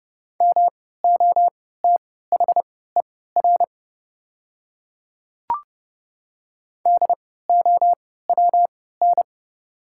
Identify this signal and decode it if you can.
Morse code, heard as a keyed tone, about 31 wpm, spaced as for 17 wpm, 700 Hz. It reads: MOTHER DOWN
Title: 3x Character Spacing - Top 200 Words - 30wpm